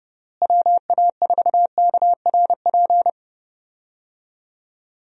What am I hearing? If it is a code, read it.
Morse code, heard as a keyed tone, about 32 wpm, 700 Hz. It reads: WA4KRP
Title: USA Call Signs - 6 Characters - Repeated 3x - 30wpm